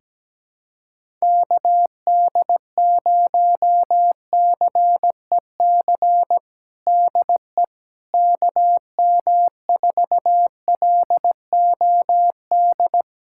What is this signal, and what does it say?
Morse code, heard as a keyed tone, about 17 wpm, 700 Hz. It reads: KD0CEC DE KM4LOD
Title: USA Callsign DE Callsign - 6 Characters - 17wpm